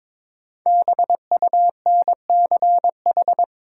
Morse code, heard as a keyed tone, about 22 wpm, 700 Hz. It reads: BUNCH
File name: 5 Letter Words - First 3 Letters Given - 22wpm